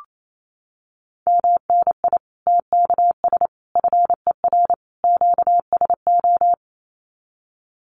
Morse code, heard as a keyed tone, about 28 wpm, 700 Hz. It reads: MNI TKS FER QSO
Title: Common QSO Phrases 28wpm